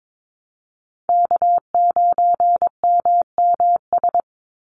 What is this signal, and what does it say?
Morse code, heard as a keyed tone, about 22 wpm, 700 Hz. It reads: K9MMS
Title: Call Signs Repeated 2x - 22wpm